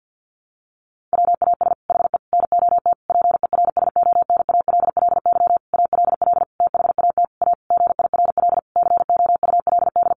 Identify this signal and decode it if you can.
Morse code, heard as a keyed tone, about 50 wpm, 700 Hz. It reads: WAS HE NOT PERSONALLY ALL THAT A GIRL COULD
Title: Sentences from Top 1400 Words 50wpm